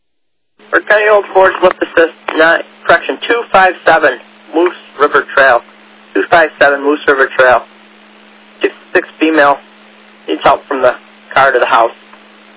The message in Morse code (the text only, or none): none